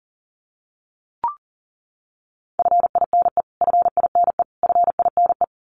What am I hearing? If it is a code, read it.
Morse code, heard as a keyed tone, about 40 wpm, 700 Hz. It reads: FINE FINE FINE
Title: Top 500 Words Repeated 40wpm